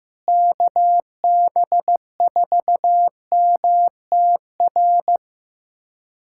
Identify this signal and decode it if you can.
Morse code, heard as a keyed tone, about 16 wpm, 700 Hz. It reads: KB4MTR
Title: USA Callsign DE Callsign 15wpm